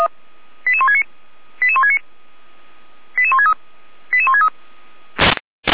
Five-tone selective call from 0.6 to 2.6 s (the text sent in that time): 88169 88169